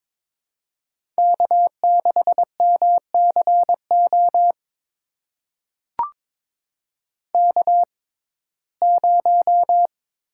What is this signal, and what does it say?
Morse code, heard as a keyed tone, about 22 wpm, spaced as for 16 wpm, 700 Hz. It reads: K6MCO K0
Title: USA Call Signs - 5 Characters - 6x Character Spacing - 22wpm